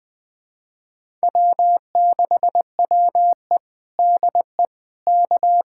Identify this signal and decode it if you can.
Morse code, heard as a keyed tone, about 21 wpm, 700 Hz. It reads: W6WE DE K
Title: USA Callsign DE Callsign - 4 Characters - 20wpm